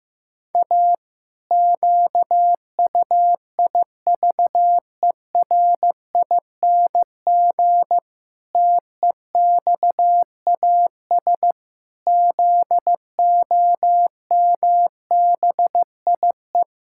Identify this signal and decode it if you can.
Morse code, heard as a keyed tone, about 15 wpm, 700 Hz. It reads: A QUIVERING TEXAS ZOMBIE
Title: Pangram Challenge v2 - 15wpm